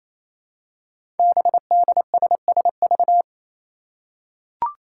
Morse code, heard as a keyed tone, about 29 wpm, 700 Hz. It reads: BDSSV
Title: Five Challenging Letter-Number Combinations 28wpm